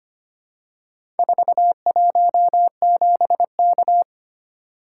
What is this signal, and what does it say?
Morse code, heard as a keyed tone, about 25 wpm, 700 Hz. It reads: 417K
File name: practice / 25wpm / Four Letter-Number Combinations - 25wpm